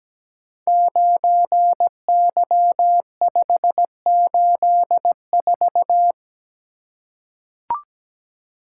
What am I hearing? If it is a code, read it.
Morse code, heard as a keyed tone, about 17 wpm, 700 Hz. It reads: 9Y584
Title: Code Groups - 5 Letter:Numbers - 17wpm